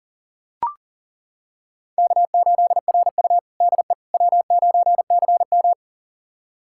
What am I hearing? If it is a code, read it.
Morse code, heard as a keyed tone, about 40 wpm, 700 Hz. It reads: K8RU DE W9CM
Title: USA Callsign DE Callsign - 4 Characters - 40wpm